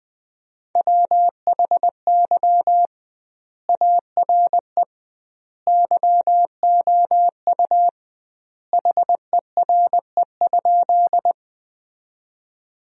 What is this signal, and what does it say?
Morse code, heard as a keyed tone, about 21 wpm, 700 Hz. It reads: WHY ARE YOU HERE?
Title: 2x Word-Spacing - Sentences from Top 200 Words - 20wpm